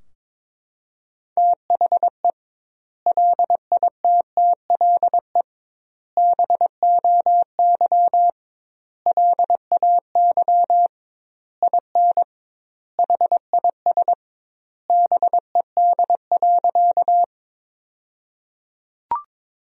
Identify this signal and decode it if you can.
Morse code, heard as a keyed tone, about 22 wpm, 700 Hz. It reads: THE LITTLE BOY LAY IN HIS BED.
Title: 2x Word-Spacing - Sentences from Top 500 Words - 22wpm